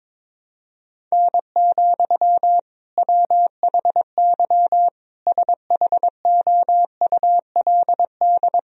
Morse code, heard as a keyed tone, about 22 wpm, 700 Hz. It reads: N, WHY SHOULD
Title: Sentences from Top 300 Words and Abbreviations 22wpm